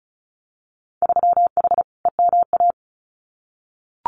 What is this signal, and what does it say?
Morse code, heard as a keyed tone, about 36 wpm, 700 Hz. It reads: 3H EMA
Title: Winter Field Day Exchanges Encore 35wpm